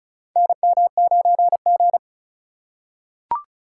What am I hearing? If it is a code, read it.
Morse code, heard as a keyed tone, about 36 wpm, 700 Hz. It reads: NM9G
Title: USA Call Signs - 4 Characters - 2x Character Spacing - 35wpm